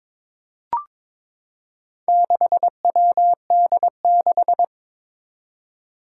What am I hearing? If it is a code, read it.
Morse code, heard as a keyed tone, about 22 wpm, 700 Hz. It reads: 6WD6
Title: Four Challenging Letter-Number Combinations 22wpm